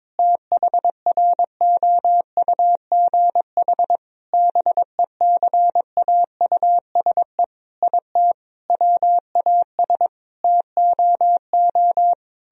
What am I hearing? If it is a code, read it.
Morse code, heard as a keyed tone, about 22 wpm, 700 Hz. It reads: THROUGH BECAUSE IT WAS TOO
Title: Sentences from Top 1000 Words - Review Part 1 - 22wpm